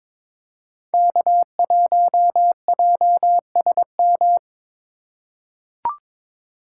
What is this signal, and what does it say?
Morse code, heard as a keyed tone, about 22 wpm, 700 Hz. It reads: K1JSM
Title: USA Call Signs - 5 Characters - 2x Character Spacing - 22wpm